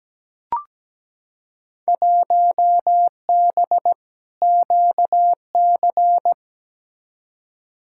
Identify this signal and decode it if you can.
Morse code, heard as a keyed tone, about 17 wpm, 700 Hz. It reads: 1B QC
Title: Field Day Exchanges Encore - 17wpm